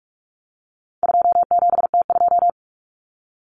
Morse code, heard as a keyed tone, about 44 wpm, 700 Hz. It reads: 27T2
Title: Common Cut Numbers - 4-Digit 45wpm